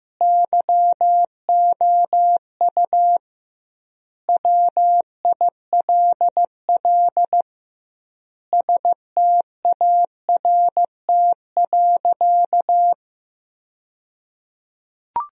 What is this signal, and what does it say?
Morse code, heard as a keyed tone, about 15 wpm, 700 Hz. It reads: YOU WILL START.
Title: 2x Word-Spacing - Sentences from Top 300 Words - 15wpm